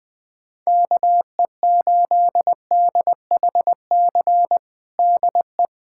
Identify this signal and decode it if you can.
Morse code, heard as a keyed tone, about 21 wpm, 690 Hz. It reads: KE8DHC DE
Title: USA Callsign DE Callsign 20wpm